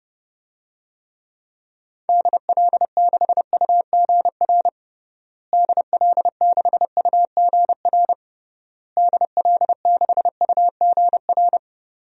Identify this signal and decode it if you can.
Morse code, heard as a keyed tone, about 32 wpm, 700 Hz. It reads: DL6UGR DL6UGR DL6UGR
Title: DXCC-100 Call Signs - 6 Characters - Repeated 3x - 30wpm